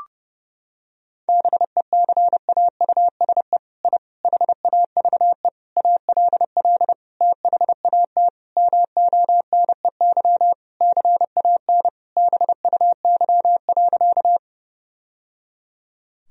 Morse code, heard as a keyed tone, about 30 wpm, 700 Hz. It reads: BECAUSE I HAVE ALL THAT MONEY CAN BUY.
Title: Sentences from Top 1000 Words - Review Part 4 - 30wpm